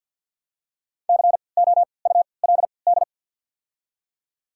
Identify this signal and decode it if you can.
Morse code, heard as a keyed tone, about 48 wpm, spaced as for 28 wpm, 700 Hz. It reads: COULD
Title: 3x Character Spacing - Top 100 Words - 50wpm